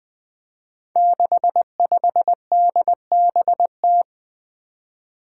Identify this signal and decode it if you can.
Morse code, heard as a keyed tone, about 21 wpm, 700 Hz. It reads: 65DBT
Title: Five Challenging Letter-Number Combinations 20wpm